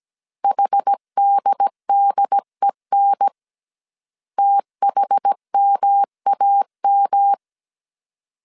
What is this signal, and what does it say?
Morse code, heard as a keyed tone, about 17 wpm, 800 Hz. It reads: HDDEN THMAM